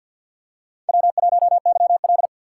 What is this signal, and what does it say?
Morse code, heard as a keyed tone, about 48 wpm, 700 Hz. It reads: W1YL
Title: USA Call Signs - 6x Character Spacing - 50wpm